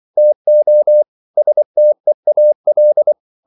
Morse code, heard as a keyed tone, about 24 wpm, 600 Hz. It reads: TO STEAL